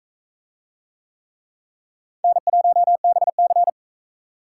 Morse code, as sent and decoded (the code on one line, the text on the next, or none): -. .---- -... -.-.
N1BC